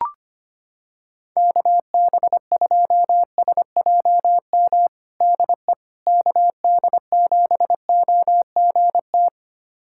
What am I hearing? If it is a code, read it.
Morse code, heard as a keyed tone, about 25 wpm, 700 Hz. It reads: KB2SJM DE KD7OGT
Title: USA Callsign DE Callsign - 6 Characters - 25wpm